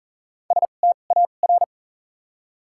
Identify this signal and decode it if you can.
Morse code, heard as a keyed tone, about 40 wpm, 700 Hz. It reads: S T A R
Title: 2x Character Spacing - Top 500 Words - 40wpm